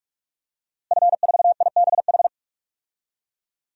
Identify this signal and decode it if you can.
Morse code, heard as a keyed tone, about 42 wpm, 700 Hz. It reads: F4IBH
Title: DXCC-100 Call Signs - 5 Characters - Repeated 2x - 45wpm